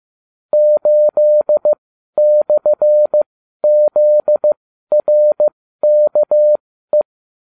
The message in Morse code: 8/ZRKE

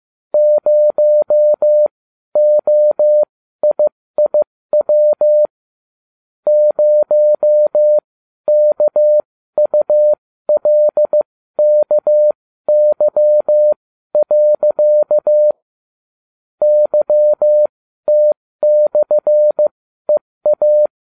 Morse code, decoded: 0OIIW 0KULKY. YT/EA

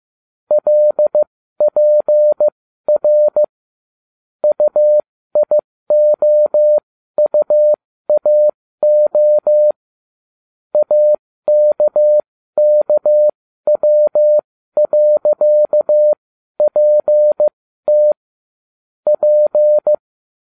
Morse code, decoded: LPR UIOUAO AKKW.PT P

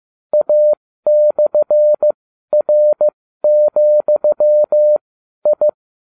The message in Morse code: A/R,I

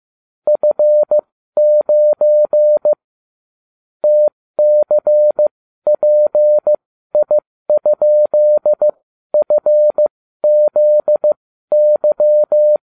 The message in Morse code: F9 TCPI?FZY

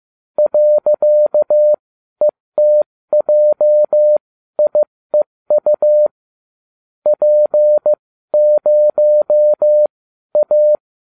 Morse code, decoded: .ETJIEU P0A

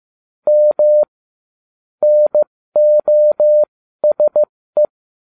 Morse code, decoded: M NOSE

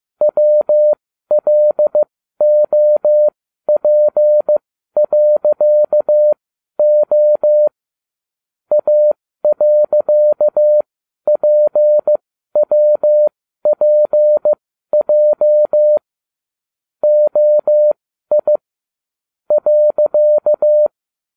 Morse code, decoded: WLOP.O A.PWPJ OI .

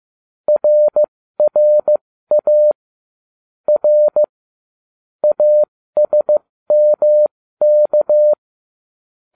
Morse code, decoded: RRA R ASMK

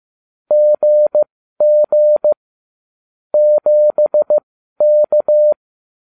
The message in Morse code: GG 7K

